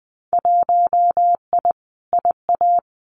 Morse code, heard as a keyed tone, about 20 wpm, 700 Hz. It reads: 1I IA
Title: Winter Field Day Exchanges 20wpm